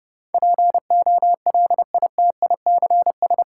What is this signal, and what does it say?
Morse code, heard as a keyed tone, about 31 wpm, 700 Hz. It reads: POLITICS
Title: Sentences from Top 1000 Words - New Word Drill - 30wpm